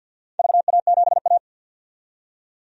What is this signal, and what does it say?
Morse code, heard as a keyed tone, about 47 wpm, 700 Hz. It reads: VA7A
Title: DXCC-100 Call Signs - 4 Characters - 50wpm